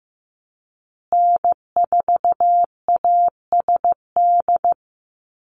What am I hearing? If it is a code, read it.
Morse code, heard as a keyed tone, about 15 wpm, 700 Hz. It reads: N4ASD
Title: USA Call Signs Repeated 2x 15wpm